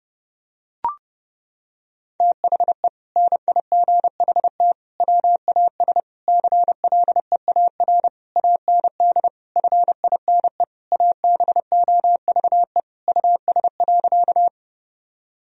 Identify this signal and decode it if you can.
Morse code, heard as a keyed tone, about 32 wpm, 700 Hz. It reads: THE NIGHT WAS CLEAR AND FINE ABOVE US.
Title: Sentences from Top 1000 Words - Review Part 2 - 30wpm